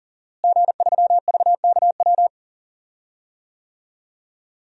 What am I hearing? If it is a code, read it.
Morse code, heard as a keyed tone, about 40 wpm, 690 Hz. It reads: G3VKW
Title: DXCC-100 Call Signs - 5 Characters - Repeated 2x - 40wpm